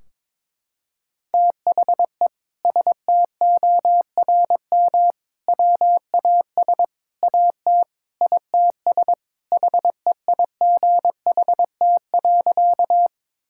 Morse code, heard as a keyed tone, about 22 wpm, 700 Hz. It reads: THE STORM WAS AT ITS HEIGHT.
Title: Sentences from Top 1600 Words 22wpm